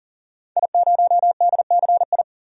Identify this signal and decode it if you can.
Morse code, heard as a keyed tone, about 40 wpm, 700 Hz. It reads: I0DCI
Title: DXCC-100 Call Signs - 5 Characters - 40wpm